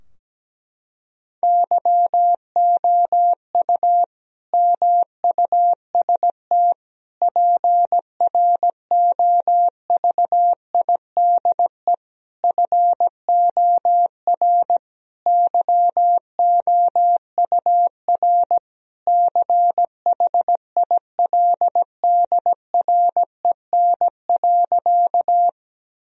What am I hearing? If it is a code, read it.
Morse code, heard as a keyed tone, about 17 wpm, 700 Hz. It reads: YOU MUST PROVIDE FOR YOUR CHILDREN.
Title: Sentences from Top 1000 Words - Review Part 3 - 17wpm